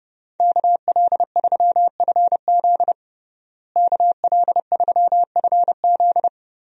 Morse code, heard as a keyed tone, about 30 wpm, 700 Hz. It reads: KL3FZ KL3FZ
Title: USA Call Signs - 5 Characters - Repeated 3x - 30wpm